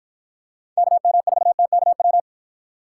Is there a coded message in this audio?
Yes: Morse code from 0.8 to 2.2 s, heard as a keyed tone, about 51 wpm, 700 Hz. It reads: KM4TKW